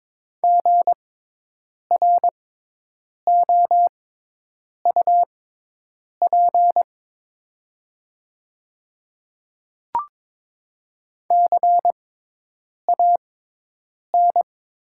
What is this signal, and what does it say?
Morse code, heard as a keyed tone, about 22 wpm, spaced as for 8 wpm, 700 Hz. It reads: GROUP CAN
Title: 6x Character Spacing - Top 300 Words - 22wpm